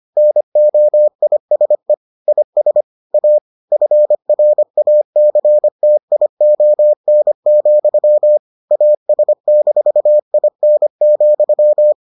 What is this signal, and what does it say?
Morse code, heard as a keyed tone, about 25 wpm, 600 Hz. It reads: NOISE IS A FRACTION, AS-IN,